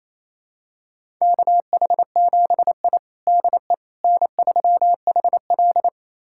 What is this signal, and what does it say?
Morse code, heard as a keyed tone, about 28 wpm, 700 Hz. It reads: KH7I DE N3HL
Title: USA Callsign DE Callsign - 4 Characters - 28wpm